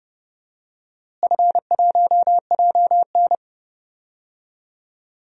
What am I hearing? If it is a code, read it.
Morse code, heard as a keyed tone, about 30 wpm, 700 Hz. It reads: F1JN
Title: DXCC-100 Call Signs - 4 Characters - Repeated 2x - 30wpm